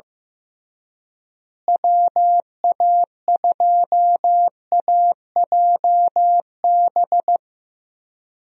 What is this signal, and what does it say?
Morse code, heard as a keyed tone, about 16 wpm, 700 Hz. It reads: WA2AJB
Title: USA Call Signs - 6 Characters - Repeated 3x - 15wpm